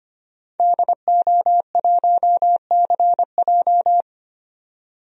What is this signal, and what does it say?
Morse code, heard as a keyed tone, about 25 wpm, 700 Hz. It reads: DO1CJ
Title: DXCC-100 Call Signs - 5 Characters - 25wpm